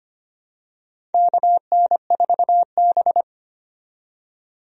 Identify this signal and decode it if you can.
Morse code, heard as a keyed tone, about 25 wpm, 700 Hz. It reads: KN4B
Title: USA Call Signs - 4 Characters - 25wpm